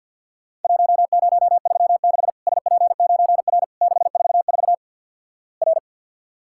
Morse code, heard as a keyed tone, about 53 wpm, 700 Hz. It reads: 1036 SP9R 644 R
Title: Worked All Europe DX Contest - QTC Exchange with Repeats - 50wpm